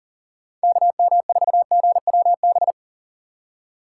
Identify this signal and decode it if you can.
Morse code, heard as a keyed tone, about 40 wpm, 700 Hz. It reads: KM4GWB
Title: USA Call Signs - 6 Characters - 3x Character Spacing - 40wpm